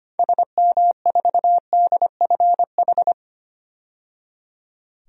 Morse code, heard as a keyed tone, about 25 wpm, 700 Hz. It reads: SM4DFH
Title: DXCC-100 Call Signs - 6 Characters - 25wpm